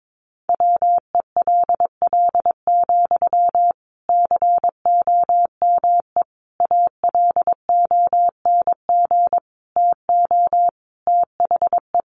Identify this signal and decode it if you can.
Morse code, heard as a keyed tone, about 22 wpm, 700 Hz. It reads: WELL, COME ALONG TO THE
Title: Sentences from Top 1300 Words 22wpm